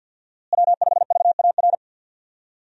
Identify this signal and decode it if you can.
Morse code, heard as a keyed tone, about 52 wpm, 700 Hz. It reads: W5VAR
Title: USA Call Signs - 5 Characters - Repeated 3x - 50wpm